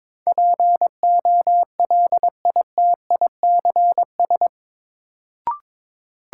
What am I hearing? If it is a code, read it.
Morse code, heard as a keyed tone, about 22 wpm, 700 Hz. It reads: POLITICS
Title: Sentences from Top 1000 Words - New Word Drill - 22wpm